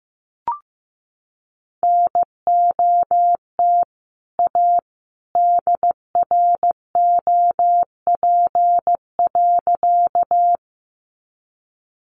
Morse code, heard as a keyed tone, about 14 wpm, 700 Hz. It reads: NOT A DROP.
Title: Sentences from Top 1400 Words 15wpm